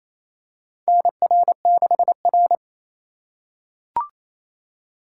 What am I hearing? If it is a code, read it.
Morse code, heard as a keyed tone, about 28 wpm, 700 Hz. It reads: NR6R
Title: USA Call Signs - 4 Characters - 4x Character Spacing - 28wpm